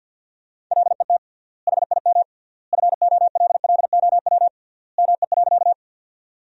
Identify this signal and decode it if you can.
Morse code, heard as a keyed tone, about 52 wpm, 700 Hz. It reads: LET HIM FOLLOW ME.
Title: 3x Word-Spacing - Sentences from Top 300 Words - 50wpm